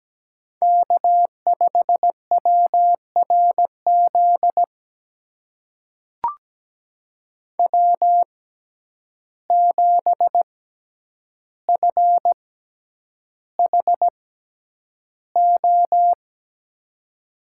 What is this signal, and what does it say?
Morse code, heard as a keyed tone, about 17 wpm, spaced as for 9 wpm, 700 Hz. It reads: K5WRZ W7FHO
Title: USA Call Signs - 5 Characters - 6x Character Spacing - 17wpm